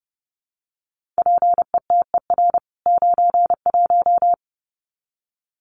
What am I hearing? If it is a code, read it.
Morse code, heard as a keyed tone, about 30 wpm, 700 Hz. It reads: PETER 91
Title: CWT Exchange 30wpm